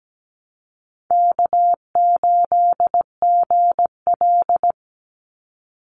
Morse code, heard as a keyed tone, about 17 wpm, 700 Hz. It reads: K8GL
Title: Easy Call Signs Repeated 2x - 17wpm